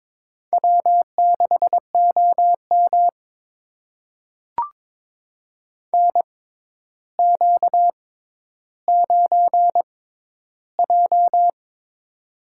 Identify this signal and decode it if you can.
Morse code, heard as a keyed tone, about 23 wpm, spaced as for 11 wpm, 700 Hz. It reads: W6OM NQ9J
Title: USA Call Signs - 4 Characters - 6x Character Spacing - 22wpm